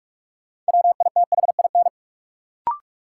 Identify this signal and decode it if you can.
Morse code, heard as a keyed tone, about 46 wpm, 700 Hz. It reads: WITHIN
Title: Sets of 2 Words - New Word Drill - 45wpm